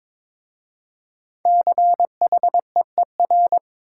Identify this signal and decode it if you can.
Morse code, heard as a keyed tone, about 22 wpm, 700 Hz. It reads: CHEER
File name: 5 Letter Words - First 3 Letters Given - 22wpm